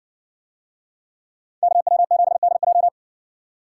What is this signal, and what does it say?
Morse code, heard as a keyed tone, about 60 wpm, 700 Hz. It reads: KW7DJ